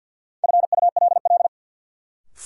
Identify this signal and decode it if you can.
Morse code, heard as a keyed tone, about 47 wpm, 700 Hz. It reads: FULL
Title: Sentences from Top 500 Words - New Word Drill - 50wpm